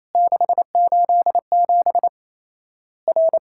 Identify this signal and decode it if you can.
Morse code, heard as a keyed tone, about 28 wpm, 700 Hz. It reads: 687 R
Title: Worked All Europe DX Contest - QTC Exchange - 28wpm